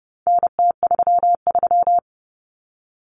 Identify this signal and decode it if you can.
Morse code, heard as a keyed tone, about 30 wpm, 700 Hz. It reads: NT33
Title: Common Cut Numbers - 4-Digit 30wpm